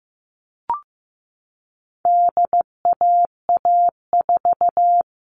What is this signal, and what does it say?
Morse code, heard as a keyed tone, about 15 wpm, 700 Hz. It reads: DAA4
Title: All Cut Numbers - 4-Digit 15wpm